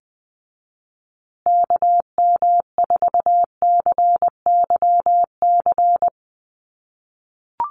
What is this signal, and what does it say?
Morse code, heard as a keyed tone, about 20 wpm, 700 Hz. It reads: KM4CYC
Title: USA Call Signs Repeated 3x 20wpm